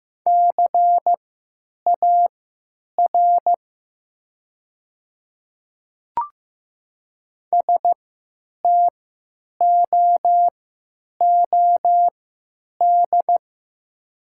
Morse code, heard as a keyed tone, about 15 wpm, spaced as for 9 wpm, 700 Hz. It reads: CAR STOOD
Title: 3x Character Spacing - Top 500 Words - 15wpm